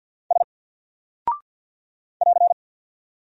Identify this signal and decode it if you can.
Morse code, heard as a keyed tone, about 48 wpm, 700 Hz. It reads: S <AR>
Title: Single Character (Letter:Number:Punctuation:Prosign) Alt 50wpm